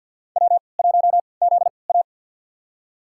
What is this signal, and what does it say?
Morse code, heard as a keyed tone, about 53 wpm, spaced as for 29 wpm, 700 Hz. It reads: W1ZA
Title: USA Call Signs - 4 Characters - 3x Character Spacing - 50wpm